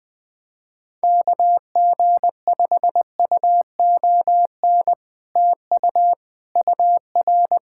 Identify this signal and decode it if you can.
Morse code, heard as a keyed tone, about 20 wpm, 700 Hz. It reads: KG5UON TU UR
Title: Speed-Racing Common QSO Phrases Encore - 20wpm